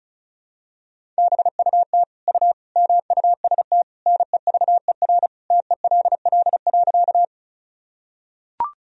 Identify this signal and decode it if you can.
Morse code, heard as a keyed tone, about 36 wpm, 700 Hz. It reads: BUT U MUST NEVER TELL.
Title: Sentences from Top 300 Words and Abbreviations 35wpm